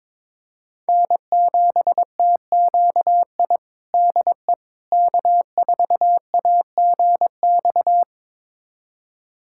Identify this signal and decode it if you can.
Morse code, heard as a keyed tone, about 22 wpm, 700 Hz. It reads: N7TQI DE K4AGX
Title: USA Callsign DE Callsign 22wpm